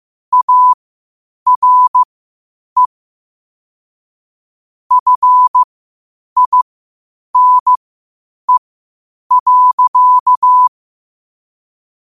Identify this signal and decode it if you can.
Morse code, heard as a keyed tone, about 15 wpm, spaced as for 8 wpm, 1000 Hz. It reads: ARE FINE.
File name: qotdfiles / 8wpm